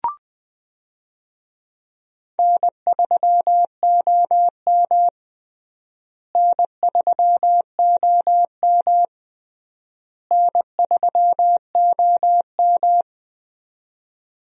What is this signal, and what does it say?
Morse code, heard as a keyed tone, about 20 wpm, 700 Hz. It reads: N3OM N3OM N3OM
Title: USA Call Signs - 4 Characters - Repeated 3x - 20wpm